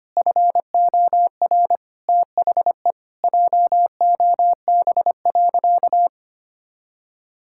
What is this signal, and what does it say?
Morse code, heard as a keyed tone, about 25 wpm, 700 Hz. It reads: FOR THE JOB.
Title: Sentences from Top 1000 Words - Review Part 6 - 25wpm